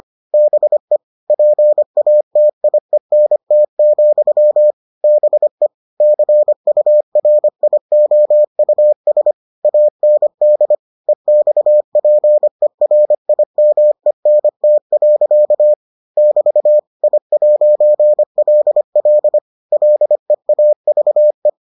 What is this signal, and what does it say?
Morse code, heard as a keyed tone, about 25 wpm, 600 Hz. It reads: BE PATIENT, BE CURIOUS AND EXPERIMENT. <BT> I'LL LEAVE